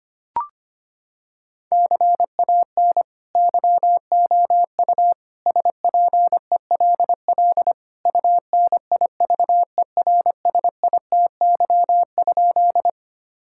Morse code, heard as a keyed tone, about 25 wpm, 700 Hz. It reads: CAN YOU SPELL UNIVERSITY?
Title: Sentences from Top 1000 Words - Review Part 5 - 25wpm